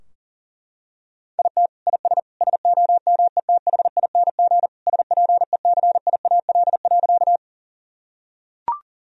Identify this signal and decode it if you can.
Morse code, heard as a keyed tone, about 40 wpm, 700 Hz. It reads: IT IS SOMETHING SPECIAL.
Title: Sentences from Top 1000 Words - Review Part 3 - 40wpm